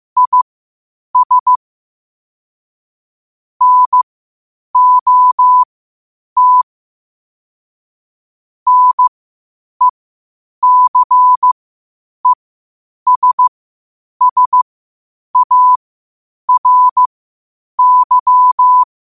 Morse code, heard as a keyed tone, about 15 wpm, spaced as for 8 wpm, 1000 Hz. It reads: IS NOT NECESSARY
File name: qotdfiles / 8wpm